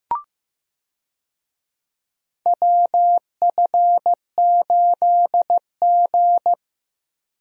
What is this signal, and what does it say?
Morse code, heard as a keyed tone, about 16 wpm, 700 Hz. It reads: WF8G
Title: USA Call Signs - 4 Characters - Repeated 3x - 15wpm